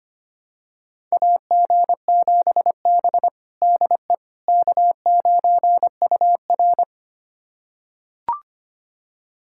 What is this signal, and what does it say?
Morse code, heard as a keyed tone, about 25 wpm, 700 Hz. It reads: AG7B DE K9UR
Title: USA Callsign DE Callsign - 4 Characters - 25wpm